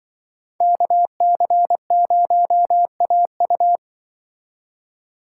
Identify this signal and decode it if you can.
Morse code, heard as a keyed tone, about 25 wpm, 700 Hz. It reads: KC0AU
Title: Speed Racing - USA 2x2 Call Signs 20wpm